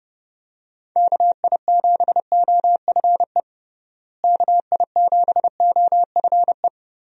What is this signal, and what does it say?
Morse code, heard as a keyed tone, about 31 wpm, 700 Hz. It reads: KI7OFE KI7OFE
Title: USA Call Signs - 6 Characters - Repeated 3x - 30wpm